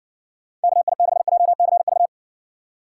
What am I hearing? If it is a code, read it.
Morse code, heard as a keyed tone, about 55 wpm, 700 Hz. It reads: KI6JCV